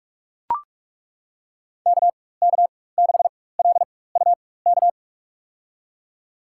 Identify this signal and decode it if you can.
Morse code, heard as a keyed tone, about 42 wpm, spaced as for 20 wpm, 700 Hz. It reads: KK6LUK
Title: USA Call Signs - 4x Character Spacing - 45wpm